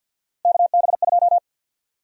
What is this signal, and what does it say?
Morse code, heard as a keyed tone, about 53 wpm, 700 Hz. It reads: KB2
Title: USA Call Sign Prefixes - 50wpm